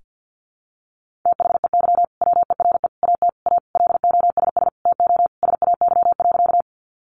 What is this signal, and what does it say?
Morse code, heard as a keyed tone, about 50 wpm, 700 Hz. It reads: THEY WERE AT A LOSS TO SAY.
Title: Sentences from Top 1400 Words 50wpm